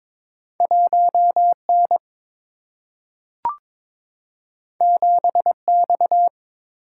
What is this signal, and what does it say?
Morse code, heard as a keyed tone, about 22 wpm, 700 Hz. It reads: 1N 7X
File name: practice / 22wpm / Two Letter-Number Combinations - 22wpm